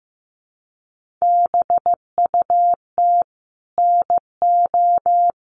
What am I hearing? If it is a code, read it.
Morse code, heard as a keyed tone, about 15 wpm, 700 Hz. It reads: BUT NO
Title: Sentences from Top 1400 Words 15wpm